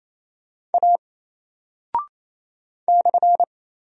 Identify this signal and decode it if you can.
Morse code, heard as a keyed tone, about 27 wpm, 700 Hz. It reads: A /
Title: Single Character (Letter:Number:Punctuation:Prosign) 28wpm